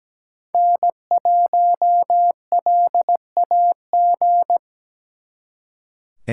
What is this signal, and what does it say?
Morse code, heard as a keyed tone, about 20 wpm, 700 Hz. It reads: N1LAG
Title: Speed Racing - USA 1x3 Call Signs 17wpm